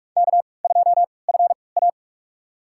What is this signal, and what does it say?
Morse code, heard as a keyed tone, about 44 wpm, spaced as for 25 wpm, 700 Hz. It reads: K2FA
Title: USA Call Signs - 4 Characters - 3x Character Spacing - 45wpm